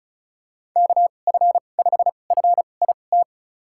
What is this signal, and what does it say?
Morse code, heard as a keyed tone, about 34 wpm, spaced as for 25 wpm, 700 Hz. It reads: KF5FIT
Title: USA Call Signs - 6 Characters - 2x Character Spacing - 35wpm